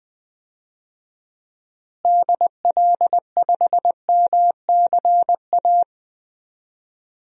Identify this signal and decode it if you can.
Morse code, heard as a keyed tone, about 20 wpm, 700 Hz. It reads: DL5MCA
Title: DXCC-100 Call Signs - 6 Characters - Repeated 2x - 20wpm